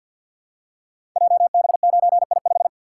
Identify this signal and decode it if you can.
Morse code, heard as a keyed tone, about 51 wpm, 700 Hz. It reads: JB9I5
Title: Code Groups - 5 Letter:Numbers - 50wpm